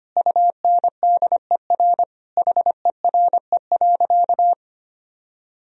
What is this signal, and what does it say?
Morse code, heard as a keyed tone, about 25 wpm, 700 Hz. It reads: UNDER HERE.
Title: Sentences from Top 1000 Words - Review Part 1 - 25wpm